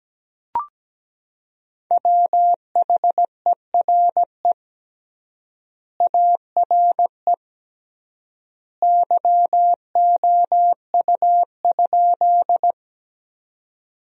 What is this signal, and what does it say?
Morse code, heard as a keyed tone, about 17 wpm, 700 Hz. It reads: WHERE ARE YOU?
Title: 3x Word-Spacing - Sentences from Top 200 Words - 17wpm